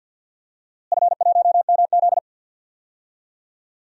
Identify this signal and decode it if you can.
Morse code, heard as a keyed tone, about 51 wpm, 700 Hz. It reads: F1MZ
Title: DXCC-100 Call Signs - 4 Characters - Repeated 3x - 50wpm